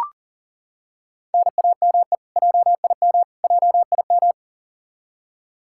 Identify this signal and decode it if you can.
Morse code, heard as a keyed tone, about 40 wpm, 700 Hz. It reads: NAME JIM JIM
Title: Practice QSO - Exchange 1 - Beginning - 40wpm